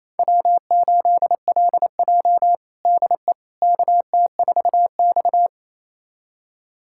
Morse code, heard as a keyed tone, about 28 wpm, 700 Hz. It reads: W8LJ DE KT4X
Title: USA Callsign DE Callsign - 4 Characters - 28wpm